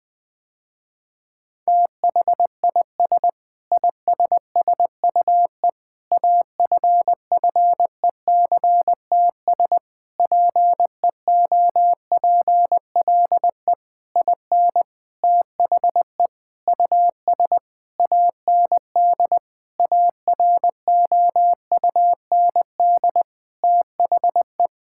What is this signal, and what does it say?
Morse code, heard as a keyed tone, about 20 wpm, 700 Hz. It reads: THIS ISSUE AFFECTS PEOPLE IN THE US AND AROUND THE
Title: Sentences with 7-Word Phrases - Spoken Beginning - 20wpm